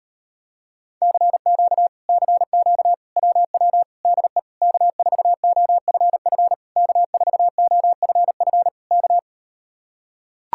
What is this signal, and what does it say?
Morse code, heard as a keyed tone, about 38 wpm, 700 Hz. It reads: CQ CQ WW DE K4OFF K4OFF K